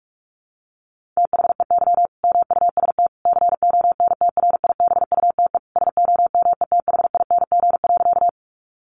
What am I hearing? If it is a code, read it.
Morse code, heard as a keyed tone, about 44 wpm, 700 Hz. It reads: THEY MUST CONTRIBUTE SOMETHING.
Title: Sentences from Top 1400 Words 45wpm